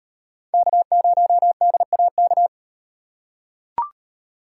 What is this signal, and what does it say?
Morse code, heard as a keyed tone, about 34 wpm, 700 Hz. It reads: K0DAK